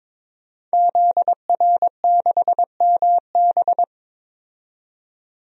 Morse code, heard as a keyed tone, about 22 wpm, 700 Hz. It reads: ZR6MB
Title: DXCC-100 Call Signs - 5 Characters - 22wpm